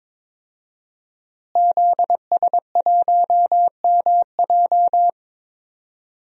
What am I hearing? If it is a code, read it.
Morse code, heard as a keyed tone, about 22 wpm, 700 Hz. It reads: ZS1MJ
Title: DXCC-100 Call Signs - 5 Characters - Repeated 2x - 22wpm